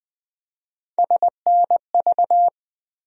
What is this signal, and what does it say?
Morse code, heard as a keyed tone, about 21 wpm, 700 Hz. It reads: SNV
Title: USA Call Sign Postfixes - 3 Letters 20wpm